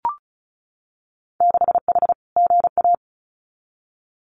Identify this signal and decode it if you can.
Morse code, heard as a keyed tone, about 36 wpm, 700 Hz. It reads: 6H GA
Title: Winter Field Day Exchanges Encore 35wpm